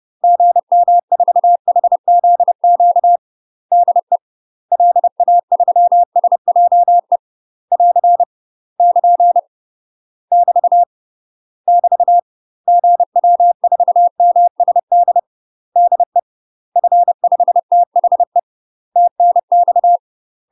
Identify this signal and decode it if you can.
Morse code, heard as a keyed tone, about 30 wpm, 700 Hz. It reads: GM4HZQ DE LA3SJE <AR> <KN> <BT> <BT> GW4MSD DE F5THE TNX
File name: UK MASTER QSOs TRAINING / 30WPM / UK MASTER QSOs 0001